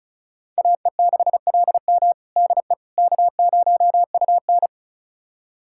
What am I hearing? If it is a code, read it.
Morse code, heard as a keyed tone, about 36 wpm, 700 Hz. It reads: AE6LM DE K0UN
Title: USA Callsign DE Callsign 35wpm